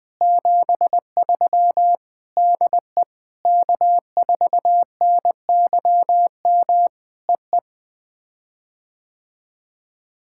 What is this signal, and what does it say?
Morse code, heard as a keyed tone, about 20 wpm, 700 Hz. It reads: 73 DE K4NYM EE
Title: POTA QSO Practice - 20wpm